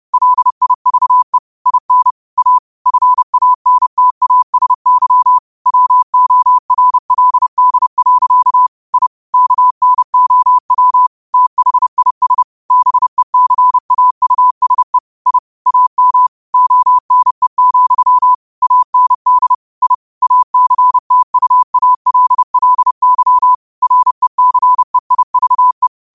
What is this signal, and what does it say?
Morse code, heard as a keyed tone, about 30 wpm, 1000 Hz. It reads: LIVE IN A FANTASY WORLD. I KNOW THIS BECAUSE I AM ONE, AND I ACTUALLY RECEIVE
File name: qotdfiles / 30wpm